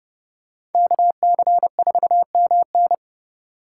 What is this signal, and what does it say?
Morse code, heard as a keyed tone, about 30 wpm, 700 Hz. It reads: KC4MN
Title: USA Call Signs - 5 Characters - 4x Character Spacing - 30wpm